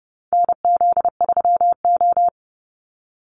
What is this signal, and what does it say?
Morse code, heard as a keyed tone, about 30 wpm, 700 Hz. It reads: NZ3O
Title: Call Signs Repeated 3x - 30wpm